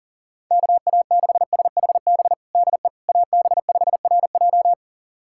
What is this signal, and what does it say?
Morse code, heard as a keyed tone, about 40 wpm, 700 Hz. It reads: KA6SHB DE AB5RJ
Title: USA Callsign DE Callsign 40wpm